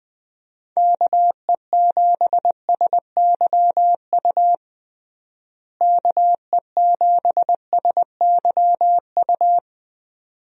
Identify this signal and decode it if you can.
Morse code, heard as a keyed tone, about 20 wpm, 700 Hz. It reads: KE7SYU KE7SYU
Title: USA Call Signs - 6 Characters - Repeated 2x - 20wpm